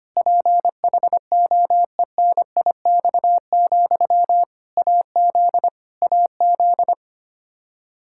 Morse code, heard as a keyed tone, about 25 wpm, 700 Hz. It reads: PHOENIX, AZ AZ
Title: K1USN SST QSO Practice 25wpm